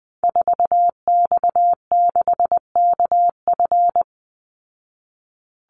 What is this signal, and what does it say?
Morse code, heard as a keyed tone, about 19 wpm, 700 Hz. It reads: 4X6KF
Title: Call Signs Repeated 2x - 20wpm